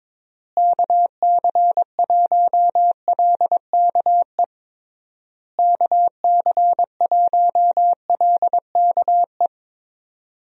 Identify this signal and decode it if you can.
Morse code, heard as a keyed tone, about 21 wpm, 700 Hz. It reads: KC1LKE KC1LKE
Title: USA Call Signs - 6 Characters - Repeated 3x - 22wpm